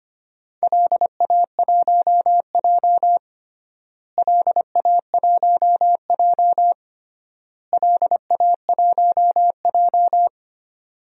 Morse code, heard as a keyed tone, about 25 wpm, 700 Hz. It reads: LA1J LA1J LA1J
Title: DXCC-100 Call Signs - 4 Characters - Repeated 3x - 25wpm